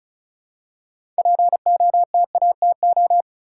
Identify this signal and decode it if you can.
Morse code, heard as a keyed tone, about 36 wpm, 700 Hz. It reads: POTATO
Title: Six Letter Words - 35wpm